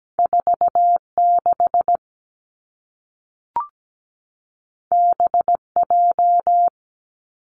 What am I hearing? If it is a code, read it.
Morse code, heard as a keyed tone, about 17 wpm, 700 Hz. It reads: 4 6 B J
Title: Two Challenging Letter-Number Combinations 17wpm